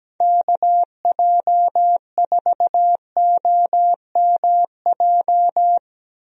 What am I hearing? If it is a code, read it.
Morse code, heard as a keyed tone, about 17 wpm, 700 Hz. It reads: KJ4OMJ
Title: USA Call Signs - 6 Characters - Repeated 2x - 17wpm